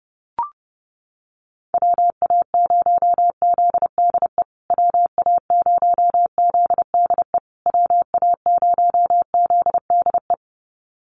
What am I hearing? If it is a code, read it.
Morse code, heard as a keyed tone, about 30 wpm, 700 Hz. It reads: WA0ZDE WA0ZDE WA0ZDE
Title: USA Call Signs Repeated 3x 30wpm